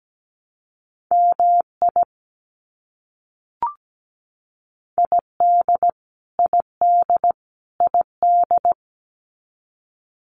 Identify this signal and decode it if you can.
Morse code, heard as a keyed tone, about 17 wpm, 700 Hz. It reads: MI ID ID ID
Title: US States Repeated 3x 17wpm